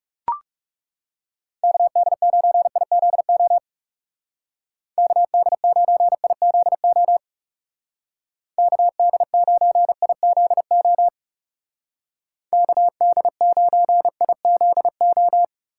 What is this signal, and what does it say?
Morse code, heard as a keyed tone, about 36 wpm, 700 Hz. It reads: KD9IZO KD9IZO KD9IZO KD9IZO
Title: Speed Racing - USA 2x3 Call Signs 30wpm